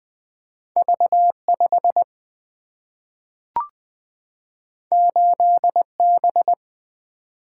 Morse code, heard as a keyed tone, about 20 wpm, 700 Hz. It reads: V 5 8 B
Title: Two Letter-Number Combinations - 20wpm